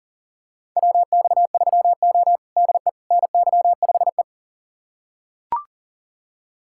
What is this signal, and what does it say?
Morse code, heard as a keyed tone, about 37 wpm, 700 Hz. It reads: WX3O DE NY5E
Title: USA Callsign DE Callsign - 4 Characters - 40wpm